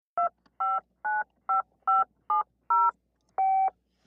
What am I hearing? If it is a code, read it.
DTMF, heard as telephone touch tones, tones of about 180 ms. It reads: 2595570
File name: functions / downloadSound